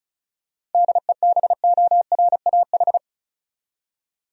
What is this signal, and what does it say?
Morse code, heard as a keyed tone, about 36 wpm, 700 Hz. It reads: DEBORAH
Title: USA Common Names - 35wpm